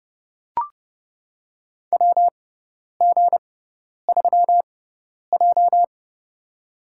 Morse code, heard as a keyed tone, about 30 wpm, spaced as for 10 wpm, 700 Hz. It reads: WG3J
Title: USA Call Signs - 4 Characters - 6x Character Spacing - 30wpm